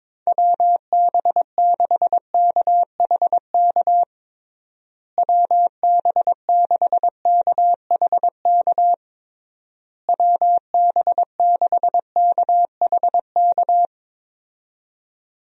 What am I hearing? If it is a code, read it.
Morse code, heard as a keyed tone, about 22 wpm, 700 Hz. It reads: WB6KHK WB6KHK WB6KHK
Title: USA Call Signs - 6 Characters - Repeated 3x - 22wpm